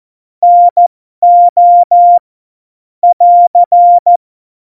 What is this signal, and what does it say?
Morse code, heard as a keyed tone, about 14 wpm, spaced as for 12 wpm, 700 Hz. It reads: NO <AR>
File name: Morsekurs / Witze